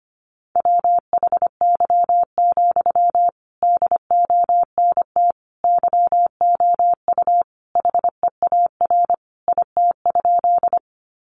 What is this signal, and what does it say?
Morse code, heard as a keyed tone, about 25 wpm, 700 Hz. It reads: WHY, DONT YOU HEAR IT?
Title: Sentences from Top 300 Words 25wpm